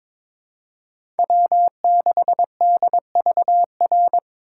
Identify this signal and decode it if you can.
Morse code, heard as a keyed tone, about 22 wpm, 700 Hz. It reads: W6DVR
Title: USA Call Signs - 2x Character Spacing - 22wpm